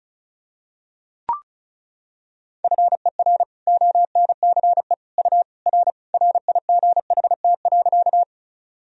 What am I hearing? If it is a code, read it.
Morse code, heard as a keyed tone, about 36 wpm, 700 Hz. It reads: FER ONCE U R RIGHT.
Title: Sentences from Top 300 Words and Abbreviations 35wpm